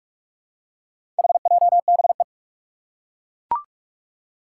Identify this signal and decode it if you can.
Morse code, heard as a keyed tone, about 41 wpm, 700 Hz. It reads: HJBE
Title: Four Challenging Letter-Number Combinations 45wpm